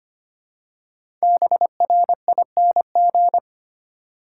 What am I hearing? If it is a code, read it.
Morse code, heard as a keyed tone, about 25 wpm, 700 Hz. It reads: BRING
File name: 5 Letter Words - First 3 Letters Given - 25wpm